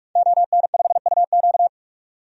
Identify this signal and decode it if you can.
Morse code, heard as a keyed tone, about 46 wpm, 700 Hz. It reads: ON5UQ
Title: DXCC-100 Call Signs - 5 Characters - 45wpm